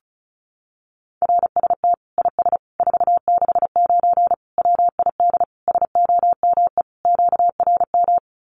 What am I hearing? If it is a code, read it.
Morse code, heard as a keyed tone, about 34 wpm, 700 Hz. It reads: RST IS 469 WID SOME QRM
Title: Common QSO Phrases 35wpm